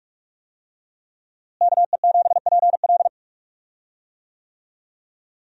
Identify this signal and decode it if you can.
Morse code, heard as a keyed tone, about 46 wpm, 700 Hz. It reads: KE7PL
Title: USA Call Signs - 5 Characters - Repeated 2x - 45wpm